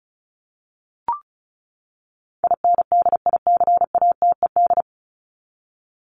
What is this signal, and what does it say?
Morse code, heard as a keyed tone, about 34 wpm, 700 Hz. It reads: INDICATED
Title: ED Suffix - Top 100 Words with Base Words 35wpm